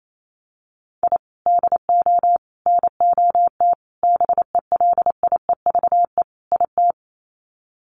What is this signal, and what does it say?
Morse code, heard as a keyed tone, about 28 wpm, 700 Hz. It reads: I DO NOT BELIEVE IT
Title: Sets of 5 Words 28wpm